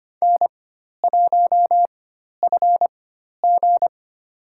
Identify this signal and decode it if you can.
Morse code, heard as a keyed tone, about 25 wpm, spaced as for 12 wpm, 700 Hz. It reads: N1FG
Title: USA Call Signs - 4 Characters - 4x Character Spacing - 25wpm